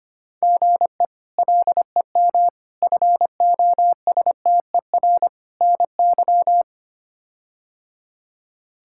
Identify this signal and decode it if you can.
Morse code, heard as a keyed tone, about 25 wpm, 700 Hz. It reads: GE LEM FOSTER NY
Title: K1USN SST QSO Practice - 25wpm